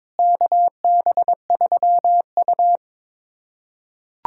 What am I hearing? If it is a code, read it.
Morse code, heard as a keyed tone, about 22 wpm, 700 Hz. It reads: KB3U
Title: USA Callsign DE Callsign - 4 Characters - 22wpm